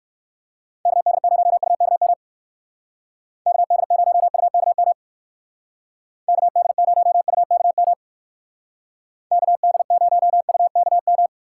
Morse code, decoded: KD0UKM KD0UKM KD0UKM KD0UKM

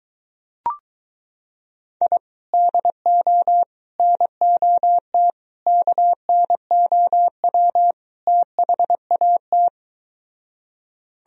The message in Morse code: I DO NOT KNOW THAT

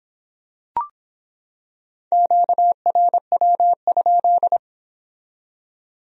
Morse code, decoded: QRW?

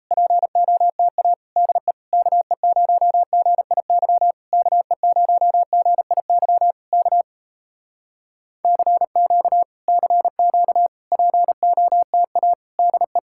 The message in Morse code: POTA DE KE0GIY KE0GIY K CQ CQ POTA DE

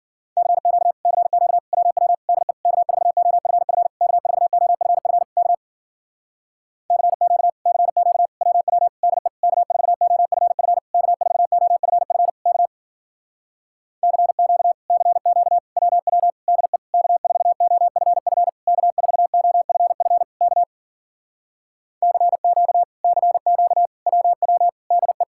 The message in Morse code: CQ CQ WW DE K4OFF K4OFF K CQ CQ WW DE K4OFF K4OFF K CQ CQ WW DE K4OFF K4OFF K CQ CQ WW DE

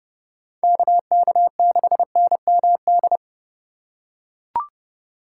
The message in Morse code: KK6NMD

